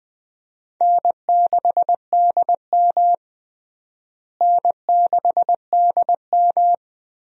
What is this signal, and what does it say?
Morse code, heard as a keyed tone, about 20 wpm, 700 Hz. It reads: N6DM N6DM
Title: USA Call Signs - 4 Characters - Repeated 3x - 20wpm